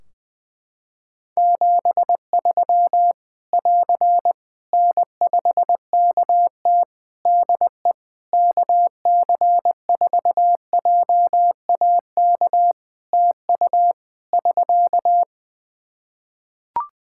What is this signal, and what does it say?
Morse code, heard as a keyed tone, about 21 wpm, 700 Hz. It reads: 73 <AR> N5KT DE KC4JAK TU <SK>
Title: K1USN SST QSO Practice 20wpm